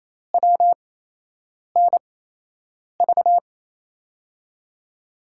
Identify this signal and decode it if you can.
Morse code, heard as a keyed tone, about 28 wpm, spaced as for 8 wpm, 700 Hz. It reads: WNV
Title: USA Call Signs - 6 Characters - 8x Character Spacing - 28wpm